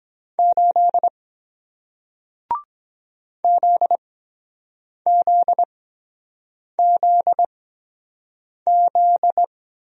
Morse code, heard as a keyed tone, about 23 wpm, spaced as for 7 wpm, 700 Hz. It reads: 8 ZZZZ